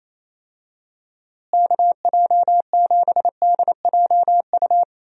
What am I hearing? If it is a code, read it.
Morse code, heard as a keyed tone, about 27 wpm, 700 Hz. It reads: KJ7DJU
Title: USA Call Signs - 6 Characters - Repeated 3x - 28wpm